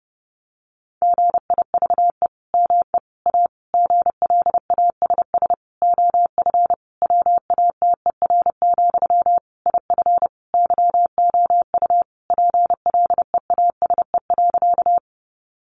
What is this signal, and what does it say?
Morse code, heard as a keyed tone, about 28 wpm, 700 Hz. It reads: GIVE ME A GLASS OF WATER, IF YOU PLEASE.
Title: Sentences from Top 1500 Words 30wpm